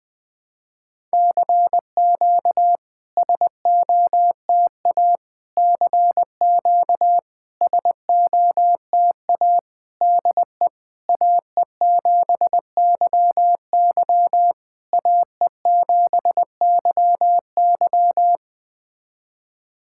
Morse code, heard as a keyed tone, about 20 wpm, 700 Hz. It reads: CQ SOTA CQ SOTA DE AE7YY AE7YY
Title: SOTA QSO Practice 20wpm